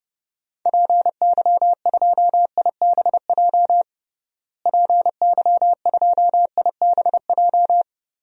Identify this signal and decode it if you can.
Morse code, heard as a keyed tone, about 30 wpm, 700 Hz. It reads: PY2IBJ PY2IBJ
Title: DXCC-100 Call Signs - 6 Characters - Repeated 3x - 30wpm